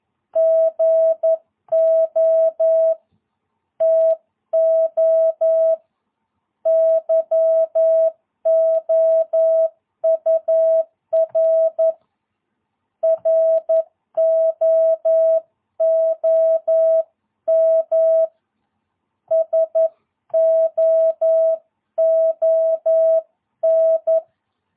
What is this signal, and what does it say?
Morse code, heard as a keyed tone, about 11 wpm, 650 Hz. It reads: GO TO YOUR ROOM SOON
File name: jh8cbh / 2022cw